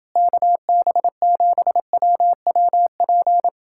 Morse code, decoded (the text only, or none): KB7WWP